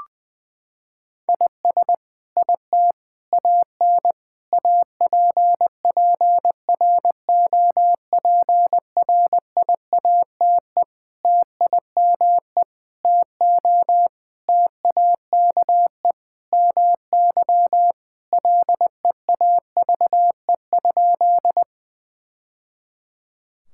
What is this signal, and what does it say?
Morse code, heard as a keyed tone, about 21 wpm, 700 Hz. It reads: IS IT AN APPROPRIATE TIME TO TAKE MY LEAVE?
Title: Sentences from Top 1000 Words - Review Part 5 - 20wpm